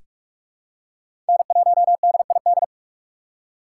Morse code, heard as a keyed tone, about 46 wpm, 700 Hz. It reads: N1DID
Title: K1USN SST QSO Practice - 45wpm